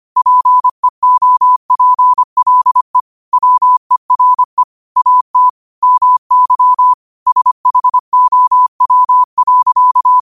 Morse code, heard as a keyed tone, about 25 wpm, 1000 Hz. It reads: PEOPLE WERE AT MY SHOW.
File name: qotdfiles / 25wpm